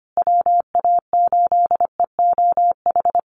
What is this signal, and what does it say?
Morse code, heard as a keyed tone, about 25 wpm, 700 Hz. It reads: WA8EOH
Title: USA Call Signs Repeated 2x 25wpm